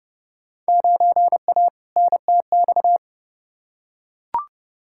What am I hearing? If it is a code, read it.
Morse code, heard as a keyed tone, about 30 wpm, 700 Hz. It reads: 9A NTX
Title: Field Day Exchanges Encore - 30wpm